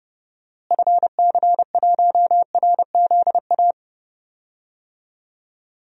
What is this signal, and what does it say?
Morse code, heard as a keyed tone, about 30 wpm, 700 Hz. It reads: FC1RZA
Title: DXCC-100 Call Signs - 6 Characters - Repeated 2x - 30wpm